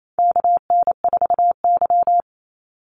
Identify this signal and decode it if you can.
Morse code, heard as a keyed tone, about 28 wpm, 700 Hz. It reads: KN4Y
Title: Call Signs Repeated 2x - 28wpm